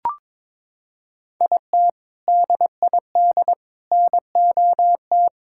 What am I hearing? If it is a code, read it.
Morse code, heard as a keyed tone, about 22 wpm, 700 Hz. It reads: IT DID NOT
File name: practice / 22wpm / Sentences with 3-Word Phrases - 22wpm